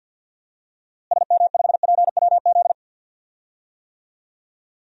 Morse code, heard as a keyed tone, about 51 wpm, 700 Hz. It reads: SM5PWZ
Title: DXCC-100 Call Signs - 6 Characters - Repeated 3x - 50wpm